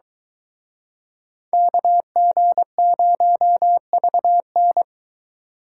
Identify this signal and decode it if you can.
Morse code, heard as a keyed tone, about 23 wpm, 700 Hz. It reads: KG0VN